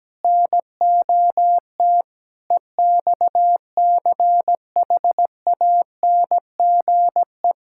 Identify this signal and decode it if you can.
Morse code, heard as a keyed tone, about 17 wpm, 700 Hz. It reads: NOT EXCHANGE
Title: Sentences from Top 1000 Words - Review Part 6 - 17wpm